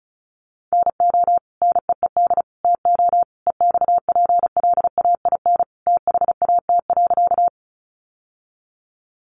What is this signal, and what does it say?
Morse code, heard as a keyed tone, about 34 wpm, 700 Hz. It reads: NO NEED TO EXPLAIN THAT.
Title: Sentences from Top 1100 Words 35wpm